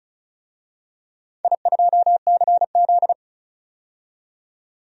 Morse code, heard as a keyed tone, about 36 wpm, 700 Hz. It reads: I2CZ
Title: DXCC-100 Call Signs - 4 Characters - 35wpm